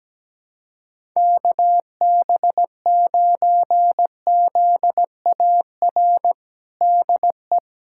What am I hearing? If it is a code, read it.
Morse code, heard as a keyed tone, about 17 wpm, 700 Hz. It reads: KB9ZAR DE
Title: USA Callsign DE Callsign - 6 Characters - 17wpm